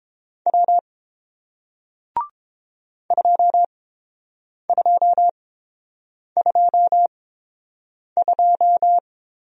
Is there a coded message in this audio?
Yes: Morse code from 0.5 to 9.0 s, heard as a keyed tone, about 29 wpm, spaced as for 7 wpm, 700 Hz. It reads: W 2222